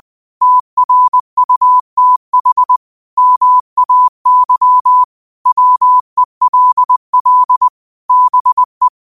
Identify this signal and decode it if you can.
Morse code, heard as a keyed tone, about 20 wpm, 1000 Hz. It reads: TRUTH MAY WELL BE
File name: qotdfiles / 20wpm